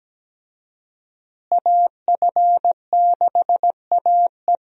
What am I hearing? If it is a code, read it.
Morse code, heard as a keyed tone, about 17 wpm, 700 Hz. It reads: AF6AE
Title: USA Call Signs - 5 Characters - Repeated 3x - 17wpm